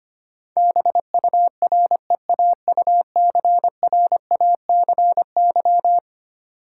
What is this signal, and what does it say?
Morse code, heard as a keyed tone, about 25 wpm, 700 Hz. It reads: BUREAUCRACY
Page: morse-code-ninja-practice-25wpm